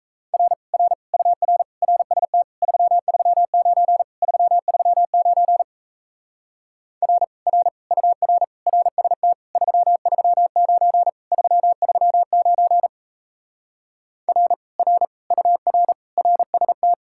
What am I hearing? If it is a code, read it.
Morse code, heard as a keyed tone, about 37 wpm, 700 Hz. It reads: R R UR RST 339 339 R R UR RST 339 339 R R UR RST